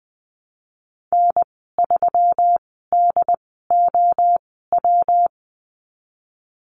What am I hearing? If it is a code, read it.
Morse code, heard as a keyed tone, about 20 wpm, spaced as for 14 wpm, 700 Hz. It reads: N3DOW
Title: 161 Lesson 26g - Letter g - Call Signs 20wpm